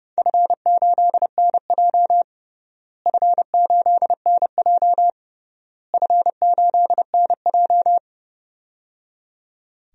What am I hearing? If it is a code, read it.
Morse code, heard as a keyed tone, about 29 wpm, 700 Hz. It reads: F8NJ F8NJ F8NJ
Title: DXCC-100 Call Signs - 4 Characters - Repeated 3x - 30wpm